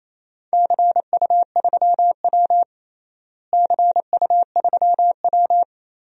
Morse code, heard as a keyed tone, about 28 wpm, 700 Hz. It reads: CU3W CU3W
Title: DXCC-100 Call Signs - 4 Characters - Repeated 3x - 28wpm